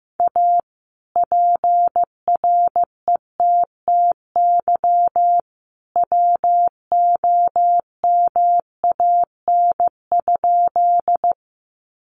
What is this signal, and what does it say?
Morse code, heard as a keyed tone, about 15 wpm, 700 Hz. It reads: A PRETTY WOMAN?
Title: Sentences from Top 1100 Words 15wpm